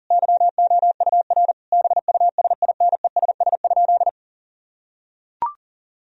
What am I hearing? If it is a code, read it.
Morse code, heard as a keyed tone, about 40 wpm, 700 Hz. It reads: YOUR BUSINESS?
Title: Sentences from Top 1000 Words - Review Part 4 - 40wpm